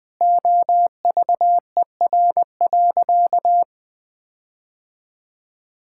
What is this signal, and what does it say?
Morse code, heard as a keyed tone, about 19 wpm, 700 Hz. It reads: OVER.
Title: Sentences with 4-Word Phrases - 20wpm